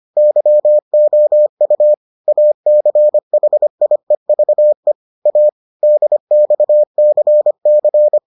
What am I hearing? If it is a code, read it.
Morse code, heard as a keyed tone, about 25 wpm, 600 Hz. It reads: YOU ACHIEVE A DXCC